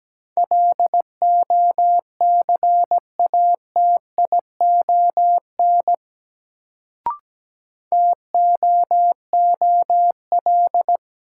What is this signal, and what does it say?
Morse code, heard as a keyed tone, about 17 wpm, 700 Hz. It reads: LOCATION TOOL
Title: Sentences from Top 1000 Words - Review Part 6 - New Word Drill - 17wpm